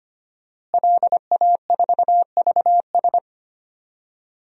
Morse code, heard as a keyed tone, about 25 wpm, 700 Hz. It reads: LA4VS
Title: DXCC-100 Call Signs - 5 Characters - Repeated 2x - 25wpm